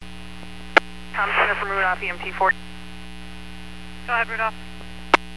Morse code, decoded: none

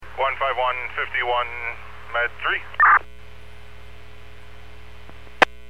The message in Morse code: none

